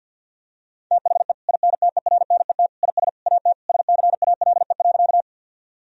Morse code, heard as a keyed tone, about 50 wpm, 700 Hz. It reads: THE INTERNET IS AT SCALE.
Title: Sentences from Top 1000 Words - Review Part 4 - 50wpm